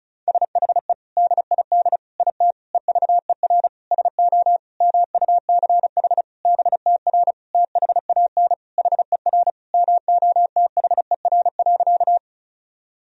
Morse code, heard as a keyed tone, about 34 wpm, 700 Hz. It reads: SHE DID IT EVER SO MUCH BTR THAN HER MOTHER.
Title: Sentences from Top 400 Words and Abbreviations 35wpm